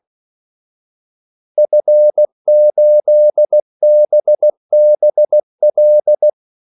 Morse code, heard as a keyed tone, about 16 wpm, 600 Hz. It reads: F8BBL
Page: Kaldesignaler 16 wpm